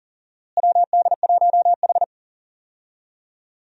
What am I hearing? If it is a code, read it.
Morse code, heard as a keyed tone, about 40 wpm, 700 Hz. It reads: WD1H
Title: USA Call Signs - 4 Characters - Repeated 2x - 40wpm